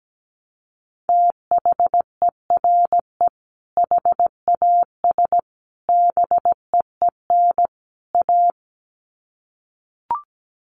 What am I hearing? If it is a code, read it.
Morse code, heard as a keyed tone, about 17 wpm, 700 Hz. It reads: THERE HAS BEEN A
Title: Sets of 4 Words 17wpm